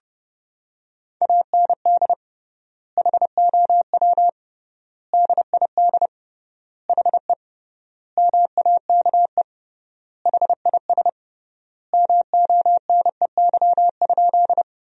Morse code, heard as a keyed tone, about 32 wpm, 700 Hz. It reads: AND HOW DID HE MAKE HIS MONEY?
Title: 3x Word-Spacing - Sentences from Top 400 Words - 30wpm